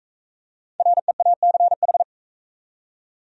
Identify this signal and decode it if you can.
Morse code, heard as a keyed tone, about 42 wpm, 700 Hz. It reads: REACH